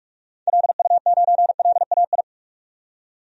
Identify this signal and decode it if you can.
Morse code, heard as a keyed tone, about 46 wpm, 700 Hz. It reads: LU9LAI
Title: DXCC-100 Call Signs - 6 Characters - 45wpm